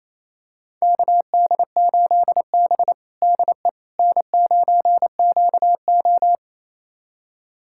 Morse code, heard as a keyed tone, about 27 wpm, 700 Hz. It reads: KD8B DE N9QO
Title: USA Callsign DE Callsign - 4 Characters - 28wpm